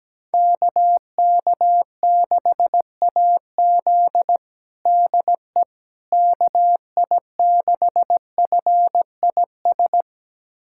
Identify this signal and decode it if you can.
Morse code, heard as a keyed tone, about 17 wpm, 700 Hz. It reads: KK6AZ DE KI6FIS
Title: USA Callsign DE Callsign 17wpm